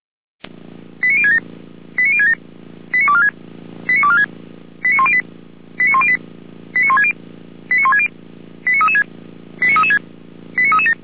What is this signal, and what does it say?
Five-tone selective call (ZVEI), heard as quick runs of five tones, tones of about 70 ms. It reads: 89067 89067 89246 89246 89108 89108 89160 89160 89336 89336 89337